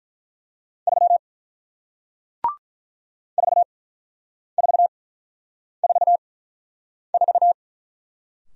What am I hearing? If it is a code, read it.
Morse code, heard as a keyed tone, about 48 wpm, spaced as for 9 wpm, 690 Hz. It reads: 3 4444